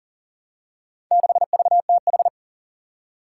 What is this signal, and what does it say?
Morse code, heard as a keyed tone, about 40 wpm, 700 Hz. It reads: 6VTH
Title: Four Challenging Letter-Number Combinations 40wpm